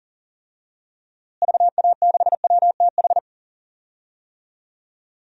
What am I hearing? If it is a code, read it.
Morse code, heard as a keyed tone, about 40 wpm, 700 Hz. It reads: VA6WTH
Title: DXCC-100 Call Signs - 6 Characters - Repeated 2x - 40wpm